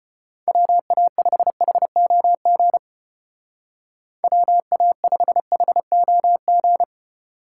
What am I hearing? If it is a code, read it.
Morse code, heard as a keyed tone, about 33 wpm, 700 Hz. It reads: WA5HOG WA5HOG